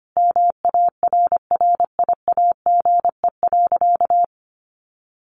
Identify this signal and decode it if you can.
Morse code, heard as a keyed tone, about 25 wpm, 700 Hz. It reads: MARRIAGE.
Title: Sentences from Top 800 Words 25wpm